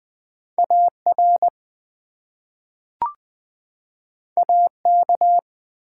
Morse code, heard as a keyed tone, about 20 wpm, 700 Hz. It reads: AR AK
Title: US States and Canadian Provinces 20wpm